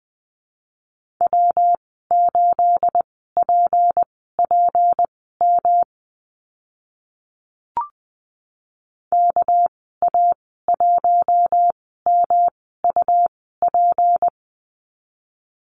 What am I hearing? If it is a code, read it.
Morse code, heard as a keyed tone, about 20 wpm, spaced as for 14 wpm, 700 Hz. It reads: W8PPM KA1MUP
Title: 231 Lesson 37g - Letter x - Call Signs 20wpm